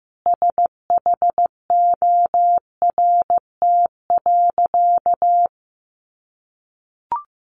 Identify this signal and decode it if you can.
Morse code, heard as a keyed tone, about 15 wpm, 700 Hz. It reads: SHORT.
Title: Sentences from Top 1400 Words 15wpm